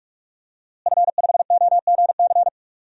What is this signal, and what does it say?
Morse code, heard as a keyed tone, about 44 wpm, 700 Hz. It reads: F5OGC
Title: DXCC-100 Call Signs - 5 Characters - Repeated 2x - 45wpm